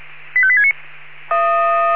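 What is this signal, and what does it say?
Five-tone selective call (ZVEI), heard as quick runs of five tones, tones of about 70 ms. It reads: 85869
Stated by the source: ZVEI five-tone call received on SDR.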